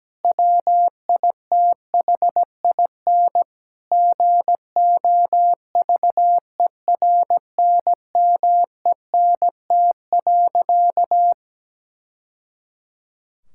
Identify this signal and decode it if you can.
Morse code, heard as a keyed tone, about 17 wpm, 700 Hz. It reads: WITHIN GOVERNMENT.
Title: Sentences from Top 1000 Words - Review Part 5 - 17wpm